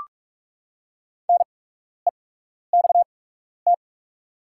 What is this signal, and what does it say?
Morse code, heard as a keyed tone, about 44 wpm, spaced as for 12 wpm, 690 Hz. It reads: NEXT
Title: 8x Character Spacing - Top 300 Words - 45wpm